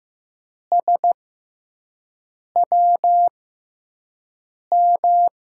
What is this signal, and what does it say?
Morse code, heard as a keyed tone, about 15 wpm, spaced as for 5 wpm, 700 Hz. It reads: SWM
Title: USA Call Signs - 5 Characters - 6x Character Spacing - 15wpm